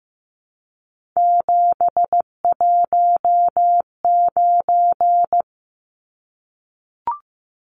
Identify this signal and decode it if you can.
Morse code, heard as a keyed tone, about 15 wpm, 700 Hz. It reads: 719
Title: Numbers - 3-Digit 15wpm